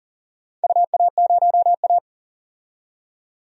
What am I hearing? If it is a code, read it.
Morse code, heard as a keyed tone, about 40 wpm, 700 Hz. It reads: UA0A
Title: DXCC-100 Call Signs - 4 Characters - 40wpm